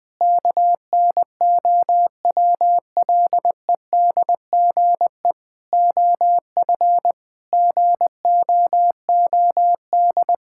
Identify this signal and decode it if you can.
Morse code, heard as a keyed tone, about 21 wpm, 700 Hz. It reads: KNOWLEDGE OF GOOD
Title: Sentences with 9-Word Phrases - Spoken Beginning - 20wpm